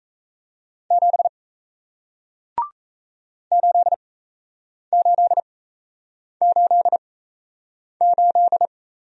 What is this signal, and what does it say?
Morse code, heard as a keyed tone, about 38 wpm, spaced as for 8 wpm, 700 Hz. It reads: 7 8888